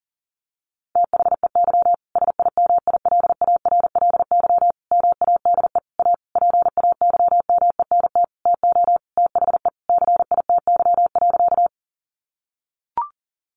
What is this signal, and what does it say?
Morse code, heard as a keyed tone, about 40 wpm, 700 Hz. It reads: THEY SIMILARLY MADE A PAYMENT TO THE CITY.
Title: Sentences from Top 1400 Words 40wpm